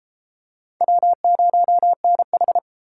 Morse code, heard as a keyed tone, about 28 wpm, 700 Hz. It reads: W0NH